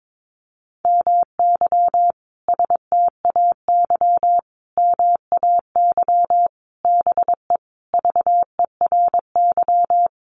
Morse code, heard as a keyed tone, about 22 wpm, 700 Hz. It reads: MY STAY MAY BE VERY
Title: Sentences from Top 1000 Words 22wpm